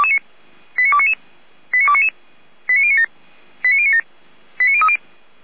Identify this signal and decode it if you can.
Five-tone selective call (ZVEI), heard as quick runs of five tones, tones of about 70 ms. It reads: 89299 89299 89097 89097 89030
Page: /455/SDR/ZVEI/